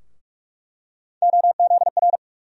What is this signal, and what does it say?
Morse code, heard as a keyed tone, about 44 wpm, 700 Hz. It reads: OZR
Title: USA Call Sign Postfixes - 3 Letters 45wpm